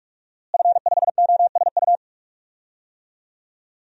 Morse code, heard as a keyed tone, about 46 wpm, 700 Hz. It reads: F5OSU
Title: DXCC-100 Call Signs - 5 Characters - Repeated 2x - 45wpm